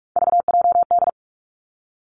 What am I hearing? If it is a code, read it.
Morse code, heard as a keyed tone, about 46 wpm, 700 Hz. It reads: VJD
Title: Three Challenging Letter-Number Combinations 45wpm